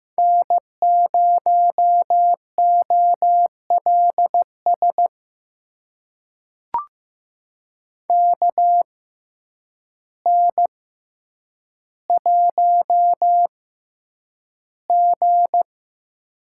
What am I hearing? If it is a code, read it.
Morse code, heard as a keyed tone, about 15 wpm, spaced as for 8 wpm, 700 Hz. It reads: N0OLS KN1G
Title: USA Call Signs - 6x Character Spacing - 15wpm